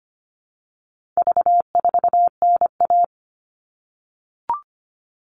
Morse code, heard as a keyed tone, about 25 wpm, 700 Hz. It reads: V4NA
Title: All Cut Numbers - 4-Digit 25wpm